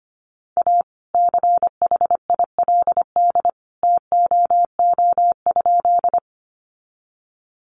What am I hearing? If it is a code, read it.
Morse code, heard as a keyed tone, about 25 wpm, 700 Hz. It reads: A CHILD TOO?
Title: Sentences from Top 800 Words 25wpm